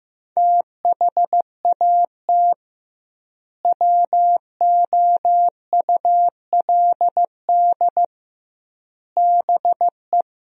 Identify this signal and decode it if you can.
Morse code, heard as a keyed tone, about 15 wpm, 700 Hz. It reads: THAT WOULD BE
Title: 2x Word-Spacing - Sentences from Top 200 Words - 15wpm